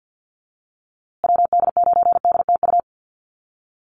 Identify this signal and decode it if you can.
Morse code, heard as a keyed tone, about 50 wpm, 700 Hz. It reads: WD9DTU
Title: USA Call Signs Repeated 3x 50wpm